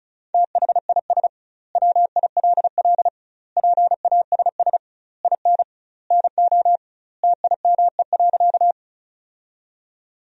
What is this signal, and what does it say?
Morse code, heard as a keyed tone, about 34 wpm, 700 Hz. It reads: THIS WILL PASS IN NO TIME.
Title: 2x Word-Spacing - Sentences from Top 400 Words - 35wpm